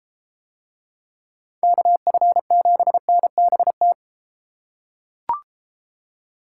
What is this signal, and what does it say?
Morse code, heard as a keyed tone, about 32 wpm, 700 Hz. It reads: KF7NBT